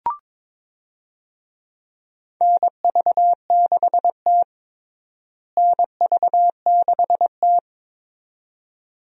Morse code, heard as a keyed tone, about 22 wpm, 700 Hz. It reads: NV6T NV6T
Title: USA Call Signs - 4 Characters - Repeated 2x - 22wpm